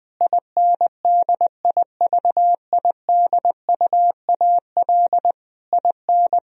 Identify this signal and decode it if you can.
Morse code, heard as a keyed tone, about 20 wpm, 700 Hz. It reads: INDIVIDUAL IN
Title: Sentences from Top 1000 Words - Review Part 4 - 20wpm